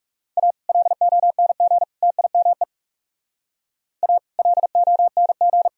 Morse code, heard as a keyed tone, about 36 wpm, 700 Hz. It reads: A LONG TIME A LONG
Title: Speed-Racing - Sets of 3 Words 30wpm